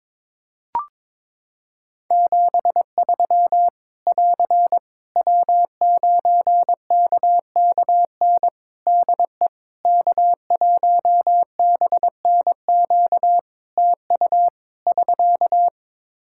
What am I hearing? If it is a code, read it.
Morse code, heard as a keyed tone, about 22 wpm, 700 Hz. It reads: 73 <AR> W9KKN DE K1BNQ TU <SK>
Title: K1USN SST QSO Practice 22wpm